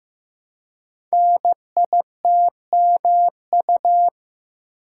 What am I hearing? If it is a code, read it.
Morse code, heard as a keyed tone, about 15 wpm, 700 Hz. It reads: NITMU
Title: Code Groups - 5 Letter:Numbers - 15wpm